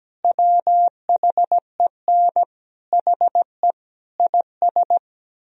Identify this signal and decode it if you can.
Morse code, heard as a keyed tone, about 17 wpm, 700 Hz. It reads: WHEN HE IS
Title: Sentences from Top 1000 Words - Review Part 4 - 17wpm